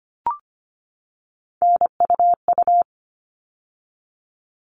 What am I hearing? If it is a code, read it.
Morse code, heard as a keyed tone, about 25 wpm, 700 Hz. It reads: NUU
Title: All Cut Numbers - 3-Digit 25wpm